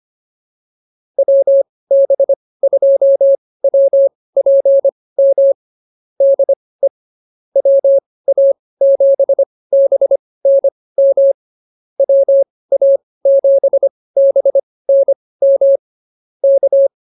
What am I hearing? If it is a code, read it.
Morse code, heard as a keyed tone, about 24 wpm, spaced as for 18 wpm, 550 Hz. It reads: WB2WPM DE WA7BNM WA7BNM K
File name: QSO106 18